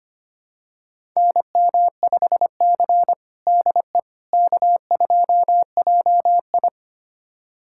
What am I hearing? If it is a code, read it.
Morse code, heard as a keyed tone, about 25 wpm, 700 Hz. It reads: NM5C DE K2JI
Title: USA Callsign DE Callsign - 4 Characters - 25wpm